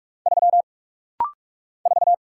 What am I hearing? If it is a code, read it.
Morse code, heard as a keyed tone, about 44 wpm, 700 Hz. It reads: 3 4
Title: Single Letter-Number - Rapid-Fire 45wpm